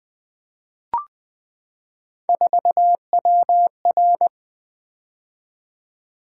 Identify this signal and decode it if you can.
Morse code, heard as a keyed tone, about 20 wpm, 700 Hz. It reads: 4WR
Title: Three Letter-Number Combinations - 20wpm